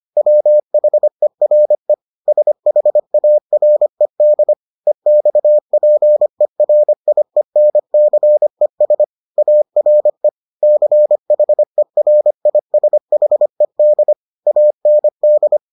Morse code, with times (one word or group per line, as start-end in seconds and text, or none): 0.2-1.9 WHERE
2.3-4.5 SHARED
4.9-9.0 EXPERIENCES
9.4-10.3 ARE
10.6-14.1 CHERISHED
14.5-15.6 AND